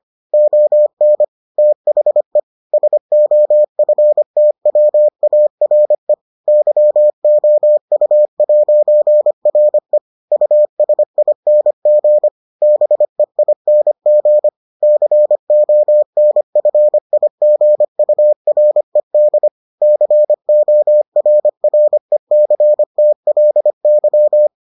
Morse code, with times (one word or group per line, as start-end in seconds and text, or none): 0.3-1.2 ON
1.6-2.4 THE
2.7-6.1 SOFTWARE
6.5-10.0 YOU'RE
10.3-12.3 USING
12.6-14.5 BEING
14.8-19.5 CONFIGURED
19.8-24.5 CORRECTLY